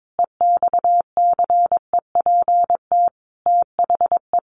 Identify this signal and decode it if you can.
Morse code, heard as a keyed tone, about 22 wpm, 700 Hz. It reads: EXCEPT THE
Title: A Sound Of Thunder 22wpm